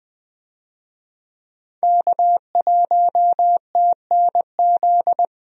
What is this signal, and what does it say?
Morse code, heard as a keyed tone, about 20 wpm, 700 Hz. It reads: K1TNZ
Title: USA Call Signs - 5 Characters - 20wpm